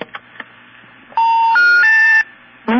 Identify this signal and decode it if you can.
Special information tone, with speech.